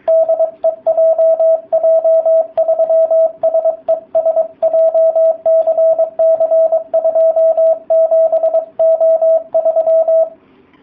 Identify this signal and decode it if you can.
Morse code, heard as a keyed tone, about 22 wpm, 660 Hz, with radio noise. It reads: DE JJ3SES JCC27O3